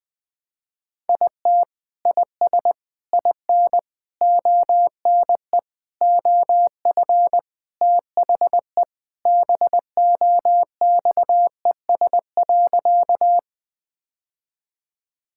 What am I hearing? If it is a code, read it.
Morse code, heard as a keyed tone, about 20 wpm, 700 Hz. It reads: IT IS IN ONE OF THE BOXES.
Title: Sentences with 3-Word Phrases - 20wpm